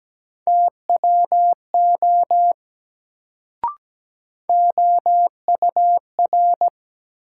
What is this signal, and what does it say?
Morse code, heard as a keyed tone, about 17 wpm, 700 Hz. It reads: TWO OUR
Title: Sets of 2 Words - New Word Drill - 17wpm